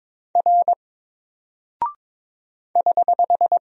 Morse code, heard as a keyed tone, about 22 wpm, 700 Hz. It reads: R <HH>
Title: Single Character (Letter:Number:Punctuation:Prosign) 22wpm